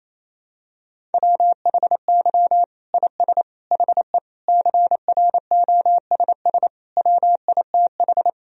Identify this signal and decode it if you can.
Morse code, heard as a keyed tone, about 28 wpm, 700 Hz. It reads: WHY IS HE CROSS WITH
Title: Sentences from Top 1000 Words - Review Part 1 - 28wpm